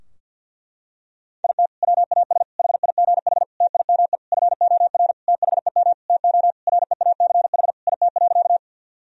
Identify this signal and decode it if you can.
Morse code, heard as a keyed tone, about 48 wpm, 700 Hz. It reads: IT WAS HIGH TIME FOR THEM TO REACH IT.
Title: Sentences from Top 1000 Words - Review Part 2 - 50wpm